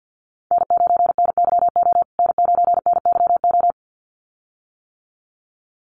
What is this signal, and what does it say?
Morse code, heard as a keyed tone, about 50 wpm, 700 Hz. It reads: N9NYO N9NYO
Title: USA Call Signs Repeated 2x 50wpm